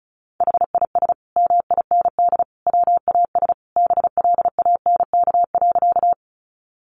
Morse code, heard as a keyed tone, about 34 wpm, 700 Hz. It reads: HIS MIND WAS BLANK.
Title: Sentences from Top 1500 Words 35wpm